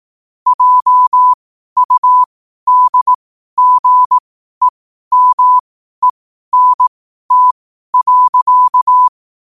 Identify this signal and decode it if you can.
Morse code, heard as a keyed tone, about 18 wpm, spaced as for 12 wpm, 1000 Hz. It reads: JUDGEMENT.